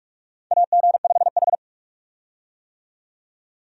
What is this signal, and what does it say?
Morse code, heard as a keyed tone, about 46 wpm, 700 Hz. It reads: AG5H
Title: USA Call Signs - 4 Characters - 45wpm